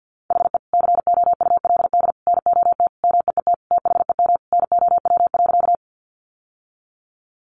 Morse code, heard as a keyed tone, about 49 wpm, 700 Hz. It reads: HE COULD NOT MEET THEM NOW.
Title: Sentences from Top 1100 Words 50wpm